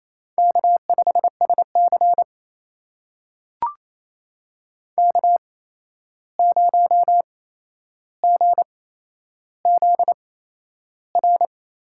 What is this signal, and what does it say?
Morse code, heard as a keyed tone, about 28 wpm, spaced as for 11 wpm, 700 Hz. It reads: K5SC K0GZR
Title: USA Call Signs - 8x Character Spacing - 28wpm